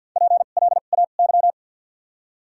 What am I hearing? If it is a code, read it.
Morse code, heard as a keyed tone, about 48 wpm, spaced as for 35 wpm, 700 Hz. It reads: PLAY
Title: 2x Character Spacing - Top 500 Words - 50wpm